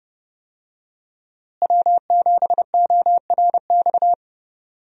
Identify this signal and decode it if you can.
Morse code, heard as a keyed tone, about 30 wpm, 700 Hz. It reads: W7ORX
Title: USA Call Signs - 5 Characters - Repeated 2x - 30wpm